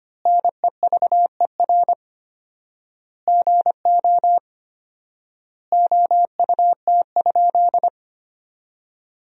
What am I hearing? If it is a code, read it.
Morse code, heard as a keyed tone, about 25 wpm, 700 Hz. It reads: NEVER GO OUT?
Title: 4x Word-Spacing - Sentences from Top 300 Words - 25wpm